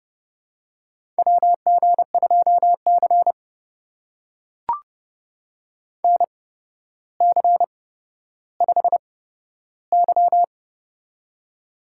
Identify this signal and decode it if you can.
Morse code, heard as a keyed tone, about 30 wpm, spaced as for 13 wpm, 700 Hz. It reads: WG2C NC5Y
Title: USA Call Signs - 4 Characters - 8x Character Spacing - 30wpm